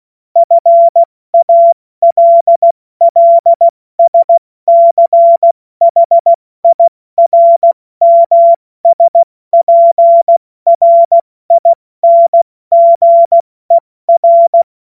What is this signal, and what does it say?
Morse code, heard as a keyed tone, about 16 wpm, 680 Hz. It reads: FALLSCHIRMSPRINGER
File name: Morsekurs / Witze / Witz 31.Oktober 2023 70 F80